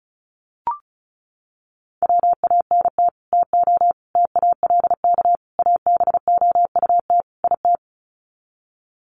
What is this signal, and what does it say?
Morse code, heard as a keyed tone, about 34 wpm, 700 Hz. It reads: WANT TO TALK ABOUT IT
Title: Sets of 5 Words Encore 35wpm